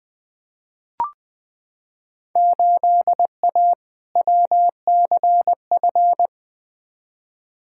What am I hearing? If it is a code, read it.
Morse code, heard as a keyed tone, about 20 wpm, 700 Hz. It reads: 8A WCF
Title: Field Day Exchanges Encore - 20wpm